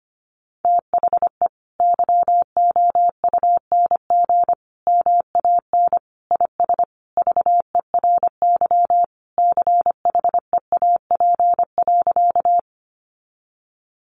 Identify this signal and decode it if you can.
Morse code, heard as a keyed tone, about 25 wpm, 700 Hz. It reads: THE YOUNG MAN IS VERY CHEAP.
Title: Sentences from Top 1200 Words 25wpm